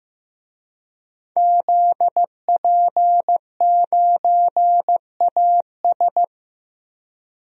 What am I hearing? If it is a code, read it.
Morse code, heard as a keyed tone, about 15 wpm, 700 Hz. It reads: ZP9AS
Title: DXCC-100 Call Signs - 5 Characters - Repeated 3x - 15wpm